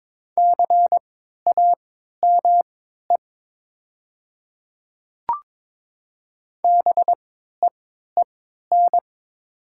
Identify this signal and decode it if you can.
Morse code, heard as a keyed tone, about 22 wpm, spaced as for 13 wpm, 690 Hz. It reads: CAME BEEN
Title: 3x Character Spacing - Top 200 Words - 22wpm